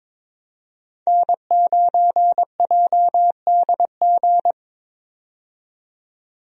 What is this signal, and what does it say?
Morse code, heard as a keyed tone, about 21 wpm, 700 Hz. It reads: N9JDG
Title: USA Call Signs - 5 Characters - 22wpm